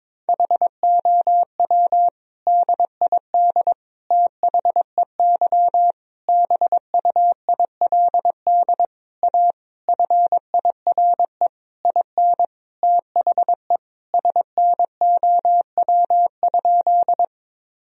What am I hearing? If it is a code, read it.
Morse code, heard as a keyed tone, about 22 wpm, 700 Hz. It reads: HOW DID THEY BUILD A FIRE IN THE SNOW?
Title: Sentences from Top 1000 Words - Review Part 3 - 22wpm